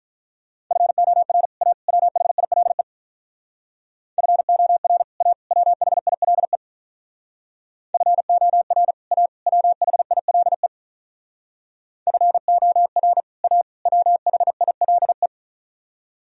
Morse code, heard as a keyed tone, about 45 wpm, 700 Hz. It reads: FOR A WHILE FOR A WHILE FOR A WHILE FOR A WHILE